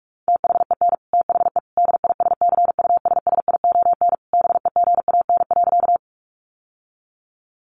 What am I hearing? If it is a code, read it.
Morse code, heard as a keyed tone, about 44 wpm, 700 Hz. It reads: THEN THE DISCUSSION BEGAN.
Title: Sentences from Top 1100 Words 45wpm